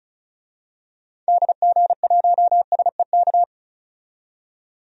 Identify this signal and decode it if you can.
Morse code, heard as a keyed tone, about 36 wpm, 700 Hz. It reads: DG1SEK
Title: DXCC-100 Call Signs - 6 Characters - 35wpm